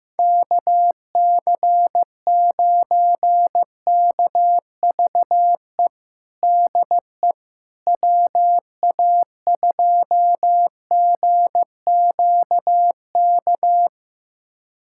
Morse code, heard as a keyed tone, about 15 wpm, 700 Hz. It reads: KC9KVE DE WA2GQK
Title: USA Callsign DE Callsign - 6 Characters - 15wpm